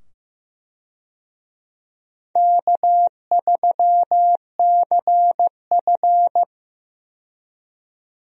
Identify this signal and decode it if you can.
Morse code, heard as a keyed tone, about 15 wpm, 700 Hz. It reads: K3CF
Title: USA Call Signs - 4 Characters - 15wpm